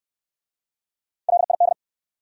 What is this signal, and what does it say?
Morse code, heard as a keyed tone, about 63 wpm, 700 Hz. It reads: LID